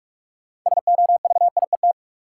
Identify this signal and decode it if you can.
Morse code, heard as a keyed tone, about 46 wpm, 700 Hz. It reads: SOVIET
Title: Sets of 5 Words Encore - New Word Drill - 45wpm